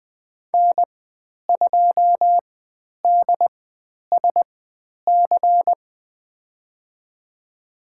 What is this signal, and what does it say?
Morse code, heard as a keyed tone, about 20 wpm, spaced as for 10 wpm, 700 Hz. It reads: N2DSC
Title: 109 Lesson 18g - Letter m - Call Signs 20wpm